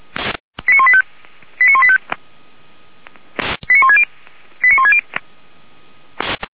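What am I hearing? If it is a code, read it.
Five-tone selective call (ZVEI), heard as quick runs of five tones, tones of about 70 ms. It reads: 89185 89185 89160 89160
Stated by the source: ZVEI five-tone call received on SDR, from the audio archive of feuerwehr-orlamuende.org (fire brigade Orlamünde).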